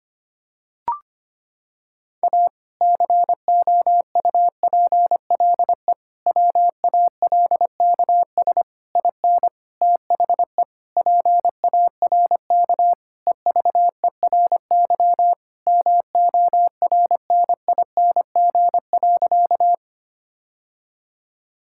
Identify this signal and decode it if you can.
Morse code, heard as a keyed tone, about 25 wpm, 700 Hz. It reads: A COUPLE WALKS IN THE PARK EVERY MORNING.
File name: practice / 25wpm / Sentences with 2-Word Phrases - 25wpm